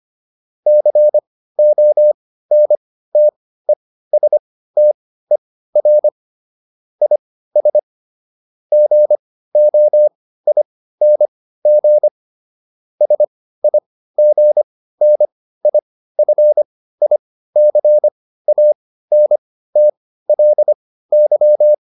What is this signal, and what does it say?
Morse code, heard as a keyed tone, about 25 wpm, spaced as for 15 wpm, 600 Hz. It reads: CONTESTER IS GOING SIGNIFICANTLY